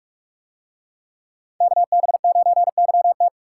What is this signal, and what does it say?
Morse code, heard as a keyed tone, about 44 wpm, 700 Hz. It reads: KB9YT